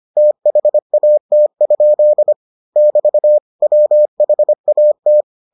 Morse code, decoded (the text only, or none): THAT? <BT> WHAT